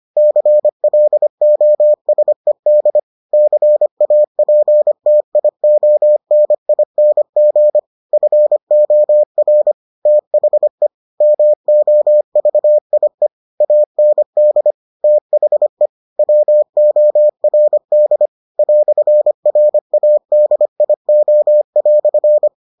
Morse code, heard as a keyed tone, about 25 wpm, 600 Hz. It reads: CLOSED CAPTIONING FOR THE MOVIE AND THE WORD "RADIO"